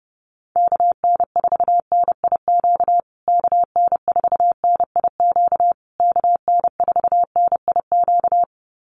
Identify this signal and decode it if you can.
Morse code, heard as a keyed tone, about 30 wpm, 700 Hz. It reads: KN4NIQ KN4NIQ KN4NIQ
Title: USA Call Signs Repeated 3x 30wpm